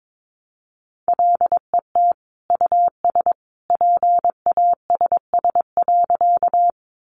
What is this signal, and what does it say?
Morse code, heard as a keyed tone, about 22 wpm, 700 Hz. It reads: LET US PASS.
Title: Sentences from Top 400 Words 22wpm